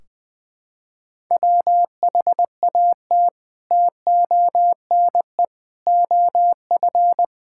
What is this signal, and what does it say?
Morse code, heard as a keyed tone, about 20 wpm, 700 Hz. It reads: WHAT TONE OF
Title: Sentences from Top 1600 Words 20wpm